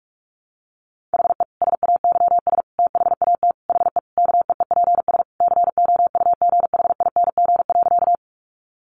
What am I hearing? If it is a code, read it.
Morse code, heard as a keyed tone, about 44 wpm, 700 Hz. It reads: HE SAYS THAT HE KEEPS COUGHING.
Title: Additional Easy Phrases 45wpm